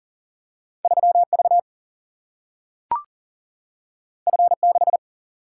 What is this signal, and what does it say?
Morse code, heard as a keyed tone, about 40 wpm, 700 Hz. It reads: 3 V F 6
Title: Single-Letter Single-Number 40wpm